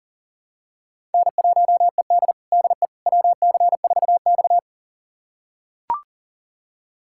Morse code, heard as a keyed tone, about 37 wpm, 700 Hz. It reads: N1ED DE WC4X
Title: USA Callsign DE Callsign - 4 Characters - 40wpm